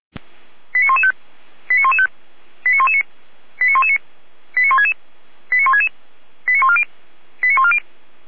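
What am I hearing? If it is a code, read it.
Five-tone selective call (ZVEI), heard as quick runs of five tones, tones of about 70 ms. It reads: 89115 89115 89119 89119 89166 89166 89140 89140
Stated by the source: SDR-received ZVEI five-tone selective call.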